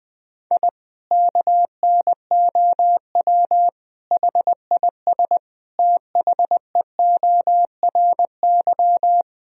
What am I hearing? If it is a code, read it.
Morse code, heard as a keyed tone, about 20 wpm, 700 Hz. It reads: I KNOW HIS THEORY
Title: Sentences from Top 1000 Words - Review Part 3 - 20wpm